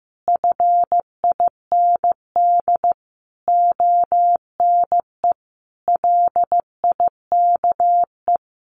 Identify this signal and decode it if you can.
Morse code, heard as a keyed tone, about 15 wpm, 700 Hz. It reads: FIND ONE LIKE
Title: Sentences from Top 1400 Words 15wpm